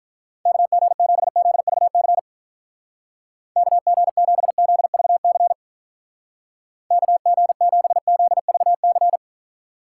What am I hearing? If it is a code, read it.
Morse code, heard as a keyed tone, about 44 wpm, 700 Hz. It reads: KG7ZVC KG7ZVC KG7ZVC
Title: Speed Racing - USA 2x3 Call Signs 35wpm